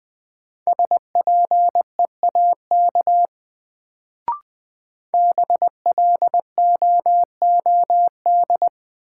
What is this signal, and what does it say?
Morse code, heard as a keyed tone, about 20 wpm, 700 Hz. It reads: SPEAK BLOOD
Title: Sentences from Top 1000 Words - Review Part 2 - New Word Drill - 20wpm